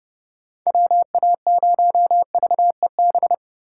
Morse code, heard as a keyed tone, about 31 wpm, 700 Hz. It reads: WA0VEB
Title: USA Call Signs - 6 Characters - 3x Character Spacing - 30wpm